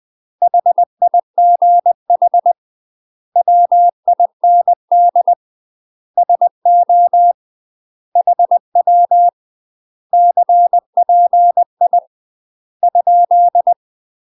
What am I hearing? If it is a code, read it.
Morse code, heard as a keyed tone, about 20 wpm, 700 Hz. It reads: HIGH WIND SO HW CPI ?
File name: UK MASTER QSOs TRAINING / 20WPM / UK MASTER QSOs 0043